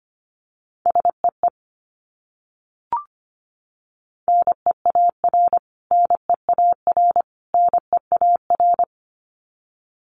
Morse code, heard as a keyed tone, about 25 wpm, 700 Hz. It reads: SEE NEAR NEAR NEAR
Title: Top 300 Words Repeated 25wpm